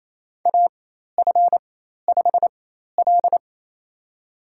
Morse code, heard as a keyed tone, about 28 wpm, spaced as for 13 wpm, 700 Hz. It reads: AF5L
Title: USA Call Signs - 4 Characters - 4x Character Spacing - 28wpm